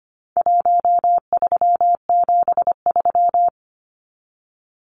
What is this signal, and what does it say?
Morse code, heard as a keyed tone, about 25 wpm, 700 Hz. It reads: 1373
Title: Numbers - Contest Style - 4-Digit 25wpm